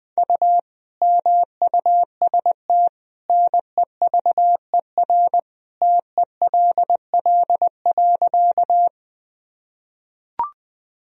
Morse code, heard as a keyed tone, about 20 wpm, 700 Hz. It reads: U MUST NEVER TELL.
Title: Sentences from Top 300 Words and Abbreviations 20wpm